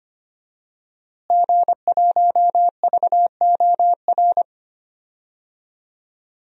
Morse code, heard as a keyed tone, about 25 wpm, 700 Hz. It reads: G1VOR
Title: DXCC-100 Call Signs - 5 Characters - Repeated 2x - 25wpm